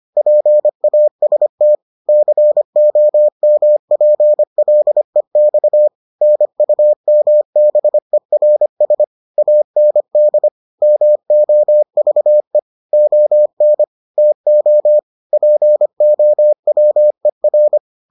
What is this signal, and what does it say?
Morse code, heard as a keyed tone, about 25 wpm, 600 Hz. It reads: PAST COMPLEX NUMBERS AND MOVE ON TO POWER